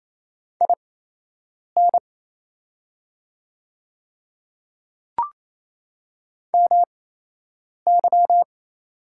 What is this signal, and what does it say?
Morse code, heard as a keyed tone, about 28 wpm, spaced as for 8 wpm, 700 Hz. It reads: IN MY
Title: 8x Character Spacing - Top 200 Words - 28wpm